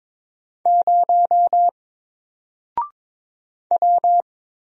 Morse code, heard as a keyed tone, about 22 wpm, 700 Hz. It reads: 0 W
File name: Single-Letter-Number-22wpm